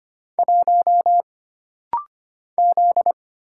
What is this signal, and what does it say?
Morse code, heard as a keyed tone, about 25 wpm, 700 Hz. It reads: 1 Z
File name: Single Character (Letter:Number:Punctuation:Prosign) - Rapid-Fire - 25wpm